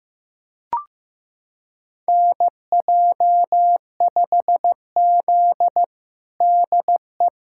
Morse code, heard as a keyed tone, about 15 wpm, 700 Hz. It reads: NJ5Z DE
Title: USA Callsign DE Callsign - 4 Characters - 15wpm